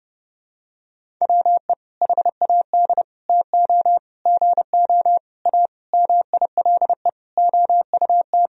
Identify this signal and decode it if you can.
Morse code, heard as a keyed tone, about 32 wpm, 700 Hz. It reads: WE HAD TO GO A MILE OUT
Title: Sentences from Top 1000 Words - Review Part 1 - 30wpm